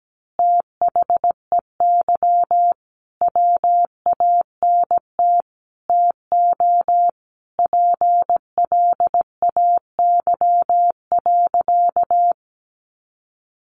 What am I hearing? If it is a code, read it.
Morse code, heard as a keyed tone, about 17 wpm, 700 Hz. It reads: THEY WANT TO PLAY.
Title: Easy Phrases 17wpm